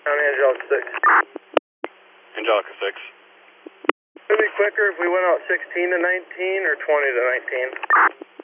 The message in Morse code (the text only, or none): none